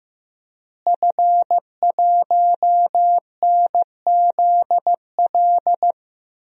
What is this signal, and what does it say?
Morse code, heard as a keyed tone, about 15 wpm, 700 Hz. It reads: F1NZL
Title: DXCC-100 Call Signs - 5 Characters - Repeated 3x - 15wpm